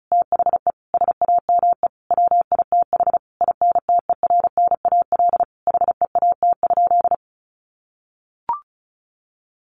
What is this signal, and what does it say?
Morse code, heard as a keyed tone, about 34 wpm, 700 Hz. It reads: THE SAME WITH INTERNAL HEAT?
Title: Sentences from Top 1400 Words 35wpm